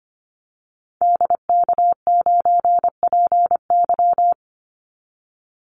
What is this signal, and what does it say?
Morse code, heard as a keyed tone, about 25 wpm, 700 Hz. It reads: DK9PY
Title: Call Signs Repeated 2x - 25wpm